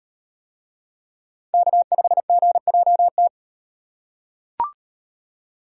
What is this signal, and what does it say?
Morse code, heard as a keyed tone, about 38 wpm, 700 Hz. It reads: K5GJT